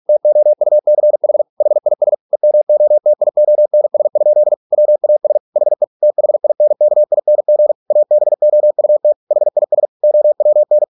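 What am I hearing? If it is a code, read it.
Morse code, heard as a keyed tone, about 46 wpm, 600 Hz. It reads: TOUCH HIS EMOTIONS? WAS HE THINKING ABOUT HIS OWN
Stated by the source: Morse practice at 45 wpm, the novel Never by Ken Follett in English, from CW med Gnister.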